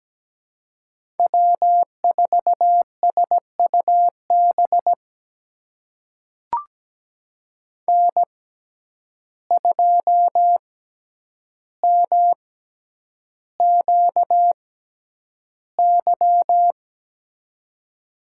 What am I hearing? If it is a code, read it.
Morse code, heard as a keyed tone, about 17 wpm, spaced as for 9 wpm, 700 Hz. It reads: W4SUB N2MQY
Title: USA Call Signs - 5 Characters - 6x Character Spacing - 17wpm